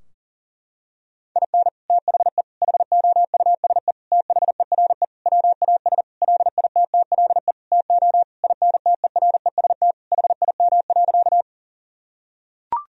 Morse code, heard as a keyed tone, about 40 wpm, 700 Hz. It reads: IN THE HOUSE THERE WAS LITTLE TO INTEREST HIM.
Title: Sentences from Top 1000 Words - Review Part 2 - 40wpm